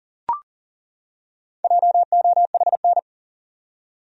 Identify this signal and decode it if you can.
Morse code, heard as a keyed tone, about 40 wpm, 700 Hz. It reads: JOHN
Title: USA Common Names - Max 5 Letters - 40wpm